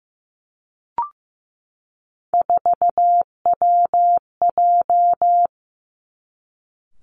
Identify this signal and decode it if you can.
Morse code, heard as a keyed tone, about 15 wpm, 700 Hz. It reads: 4WJ
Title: Three Challenging Letter-Number Combinations 15wpm